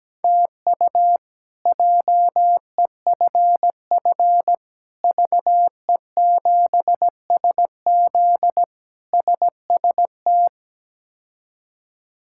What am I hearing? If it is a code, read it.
Morse code, heard as a keyed tone, about 17 wpm, 700 Hz. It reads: TU JEFF VE7SZ SST
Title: K1USN SST QSO Practice - 17wpm